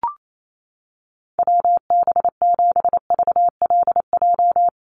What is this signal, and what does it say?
Morse code, heard as a keyed tone, about 28 wpm, 700 Hz. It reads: WB7VLJ
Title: USA Call Signs 28wpm